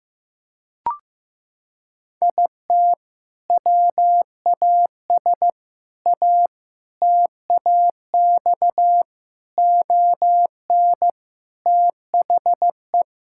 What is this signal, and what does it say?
Morse code, heard as a keyed tone, about 15 wpm, 700 Hz. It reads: IT WAS A TAX ON THE
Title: Sentences from Top 1000 Words - Review Part 5 - 15wpm